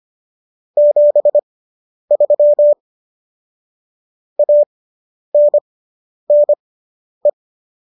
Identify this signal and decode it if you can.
Morse code, heard as a keyed tone, about 25 wpm, spaced as for 10 wpm, 590 Hz. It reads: 73 ANNE